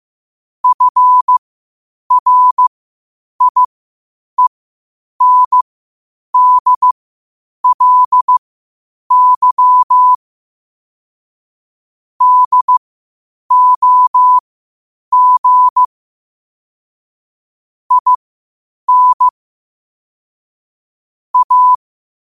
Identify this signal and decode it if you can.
Morse code, heard as a keyed tone, about 15 wpm, spaced as for 8 wpm, 1000 Hz. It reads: FRIENDLY DOG IN A
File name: qotdfiles / 8wpm